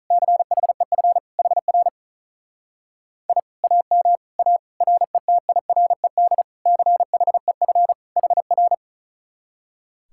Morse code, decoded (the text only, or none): CHEF HR I AM A RETIRED CHEF HR